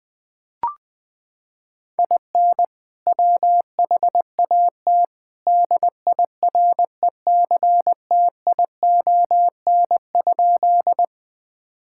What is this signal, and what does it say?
Morse code, heard as a keyed tone, about 20 wpm, 700 Hz. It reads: IN WHAT DIRECTION?
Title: Sentences from Top 1000 Words - Review Part 5 - 20wpm